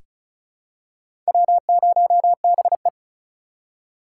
Morse code, heard as a keyed tone, about 36 wpm, 700 Hz. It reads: W0BE
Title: USA Call Signs - 3x Character Spacing - 35wpm